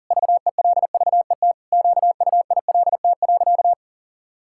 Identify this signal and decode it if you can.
Morse code, heard as a keyed tone, about 40 wpm, 700 Hz. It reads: VELVET QUILT.
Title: Pangram Challenge v2 - 40wpm